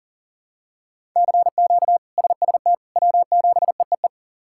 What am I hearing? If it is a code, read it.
Morse code, heard as a keyed tone, about 40 wpm, 700 Hz. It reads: CQ SST W7EEE
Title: K1USN SST QSO Practice - 40wpm